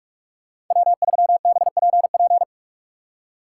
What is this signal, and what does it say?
Morse code, heard as a keyed tone, about 44 wpm, 700 Hz. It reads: W3BPP